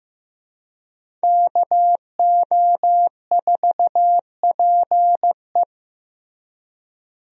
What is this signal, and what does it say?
Morse code, heard as a keyed tone, about 15 wpm, 700 Hz. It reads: KO4PE
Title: USA Call Signs - 5 Characters - Repeated 2x - 15wpm